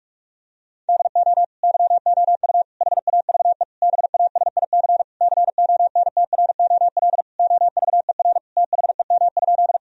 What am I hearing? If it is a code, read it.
Morse code, heard as a keyed tone, about 46 wpm, 700 Hz. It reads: DO YOU HAVE BASIC CONTROL OVER THEM?
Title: Sentences from Top 1000 Words - Review Part 3 - 45wpm